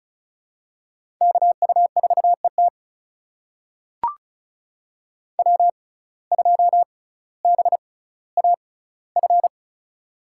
Morse code, heard as a keyed tone, about 36 wpm, spaced as for 18 wpm, 700 Hz. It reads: KU4ET W2BAF
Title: USA Call Signs - 5 Characters - 6x Character Spacing - 35wpm